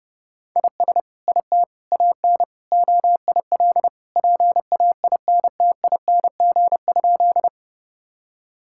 Morse code, heard as a keyed tone, about 30 wpm, 700 Hz. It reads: IS IT AN OIL PAINTING?
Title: Sentences from Top 1000 Words - Review Part 6 - 30wpm